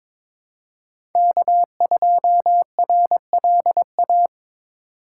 Morse code, heard as a keyed tone, about 23 wpm, 700 Hz. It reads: K2RLA
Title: Code Groups - 5 Letter:Numbers - 22wpm